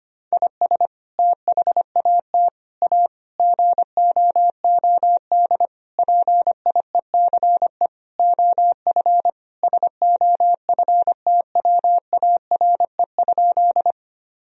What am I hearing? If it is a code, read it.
Morse code, heard as a keyed tone, about 25 wpm, 700 Hz. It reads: IS THAT A GOOD PIECE OF SOFTWARE?
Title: Sentences from Top 1000 Words - Review Part 3 - 25wpm